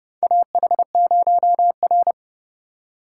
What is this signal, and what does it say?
Morse code, heard as a keyed tone, about 30 wpm, 700 Hz. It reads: AH0R
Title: USA Call Signs - 4 Characters - 3x Character Spacing - 30wpm